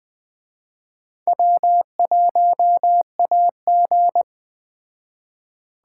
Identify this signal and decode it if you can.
Morse code, heard as a keyed tone, about 20 wpm, 700 Hz. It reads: W1AG
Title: Speed Racing - USA 1x2 Call Signs 17wpm